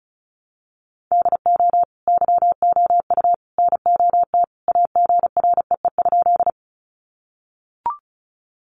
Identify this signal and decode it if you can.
Morse code, heard as a keyed tone, about 34 wpm, 700 Hz. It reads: DO YOU NOT AGREE?
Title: Sentences from Top 1100 Words 35wpm